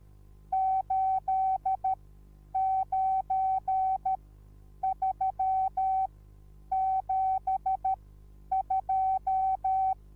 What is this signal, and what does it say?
Morse code, heard as a keyed tone, about 13 wpm, spaced as for 9 wpm, 760 Hz. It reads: 89372
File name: Onlinelehrgang / morsen / DJ4UF Morsen21